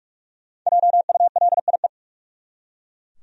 Morse code, heard as a keyed tone, about 43 wpm, 700 Hz. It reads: JULIE
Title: USA Common Names - Max 5 Letters - 45wpm